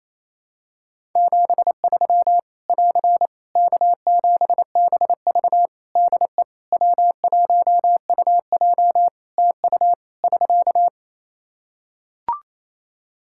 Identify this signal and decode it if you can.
Morse code, heard as a keyed tone, about 28 wpm, 700 Hz. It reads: 73 <AR> K7BV DE W1UJ TU <SK>
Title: K1USN SST QSO Practice 28wpm